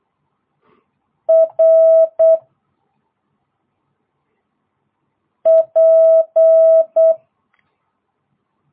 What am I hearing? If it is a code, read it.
Morse code, heard as a keyed tone, about 8 wpm, 650 Hz. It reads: R P